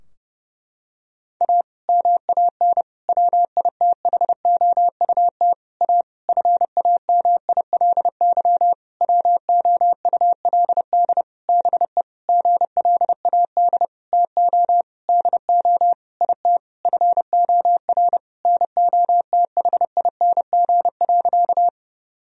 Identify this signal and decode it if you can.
Morse code, heard as a keyed tone, about 30 wpm, 700 Hz. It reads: A MAN WITHOUT A FAMILY WOULD BE GLAD TO DO IT FOR NOTHING.
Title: Sentences from Top 1000 Words - Review Part 3 - 30wpm